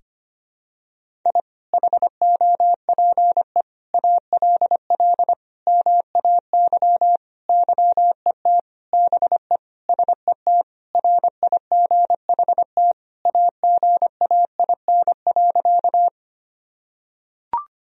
Morse code, heard as a keyed tone, about 25 wpm, 700 Hz. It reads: I HOPE ALL MAY YET BE SET RIGHT AGAIN.
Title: Sentences from Top 1000 Words - Review Part 1 - 25wpm